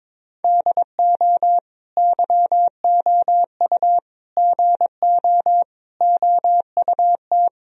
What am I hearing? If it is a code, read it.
Morse code, heard as a keyed tone, about 22 wpm, 700 Hz. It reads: DO YOU GO OUT
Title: Sentences from Top 1000 Words - Review Part 4 - 22wpm